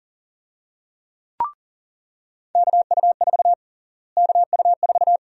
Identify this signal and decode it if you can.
Morse code, heard as a keyed tone, about 40 wpm, 700 Hz. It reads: KU4 KU4
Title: USA Call Sign Prefixes - Repeated 2x - 40wpm